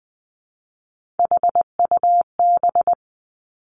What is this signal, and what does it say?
Morse code, heard as a keyed tone, about 20 wpm, 700 Hz. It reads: HUB
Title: Three Challenging Letter-Number Combinations 20wpm